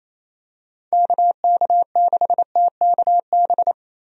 Morse code, heard as a keyed tone, about 28 wpm, 700 Hz. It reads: KK6TKB
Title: USA Call Signs - 6 Characters - Repeated 3x - 28wpm